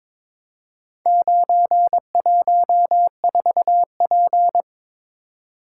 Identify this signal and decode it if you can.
Morse code, heard as a keyed tone, about 23 wpm, 700 Hz. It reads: 914P
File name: Four Letter-Number Combinations - 22wpm